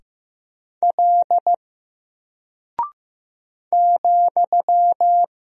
Morse code, heard as a keyed tone, about 15 wpm, 700 Hz. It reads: L ,
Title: Single Character (Letter:Number:Punctuation:Prosign) 15wpm